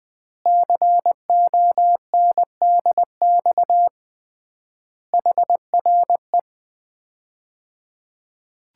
Morse code, heard as a keyed tone, about 20 wpm, 700 Hz. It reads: CONDX HRE
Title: Common QSO Phrases - 3x Word-Spacing - 20wpm